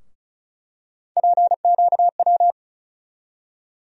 Morse code, heard as a keyed tone, about 34 wpm, 700 Hz. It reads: PQW
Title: USA Call Sign Postfixes - 3 Letters 35wpm